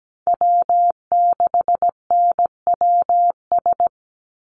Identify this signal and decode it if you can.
Morse code, heard as a keyed tone, about 17 wpm, 700 Hz. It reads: W6NWS
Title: Call Signs - 17wpm